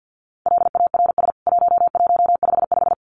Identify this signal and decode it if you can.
Morse code, heard as a keyed tone, about 48 wpm, 700 Hz. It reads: LARS 1155
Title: CWT Exchange 50wpm